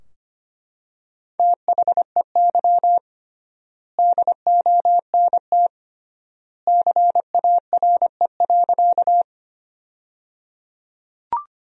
Morse code, heard as a keyed tone, about 25 wpm, 700 Hz. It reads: THEY DONT CARE.
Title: 3x Word-Spacing - Sentences from Top 300 Words - 25wpm